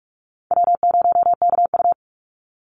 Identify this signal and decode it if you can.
Morse code, heard as a keyed tone, about 44 wpm, 700 Hz. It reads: W0KU
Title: Easy Call Signs Repeated 3x - 45wpm